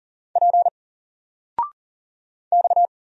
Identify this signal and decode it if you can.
Morse code, heard as a keyed tone, about 40 wpm, 700 Hz. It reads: P X
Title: Single Letter-Number 40wpm